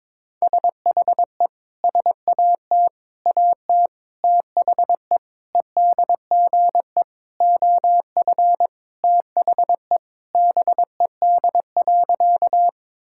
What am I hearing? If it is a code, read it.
Morse code, heard as a keyed tone, about 22 wpm, 700 Hz. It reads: SHE SAT AT THE EDGE OF THE BED.
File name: practice / 22wpm / Sentences with 4-Word Phrases - 22wpm